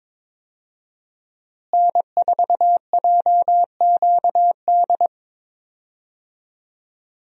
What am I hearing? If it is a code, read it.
Morse code, heard as a keyed tone, about 22 wpm, 700 Hz. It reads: N4JQD
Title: USA Call Signs - 5 Characters - Repeated 3x - 22wpm